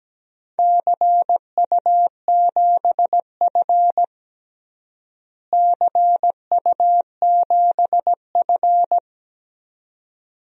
Morse code, heard as a keyed tone, about 17 wpm, 700 Hz. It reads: CU7F CU7F
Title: DXCC-100 Call Signs - 4 Characters - Repeated 2x - 17wpm